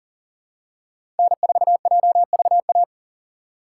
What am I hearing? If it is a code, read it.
Morse code, heard as a keyed tone, about 40 wpm, 700 Hz. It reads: N4JVA
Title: USA Call Signs - 5 Characters - 2x Character Spacing - 40wpm